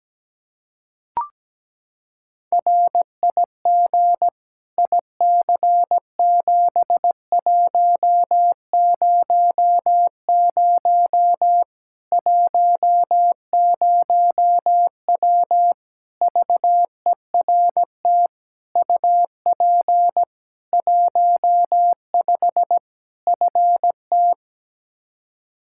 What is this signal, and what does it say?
Morse code, heard as a keyed tone, about 17 wpm, 700 Hz. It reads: RIG IC7100 10W VERT UP 15 FT
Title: Practice QSO - Exchange 2 - Info - 17wpm